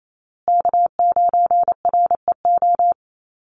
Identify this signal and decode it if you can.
Morse code, heard as a keyed tone, about 28 wpm, 700 Hz. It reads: K9REO
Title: USA Call Signs Repeated 3x 28wpm